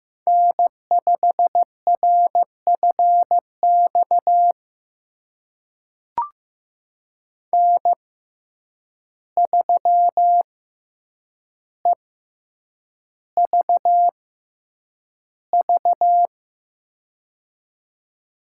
Morse code, heard as a keyed tone, about 15 wpm, spaced as for 8 wpm, 700 Hz. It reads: N5RFX N3EVV
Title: USA Call Signs - 5 Characters - 6x Character Spacing - 15wpm